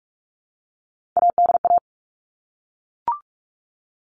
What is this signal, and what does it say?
Morse code, heard as a keyed tone, about 44 wpm, 700 Hz. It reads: ADA
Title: All Cut Numbers - 3-Digit 45wpm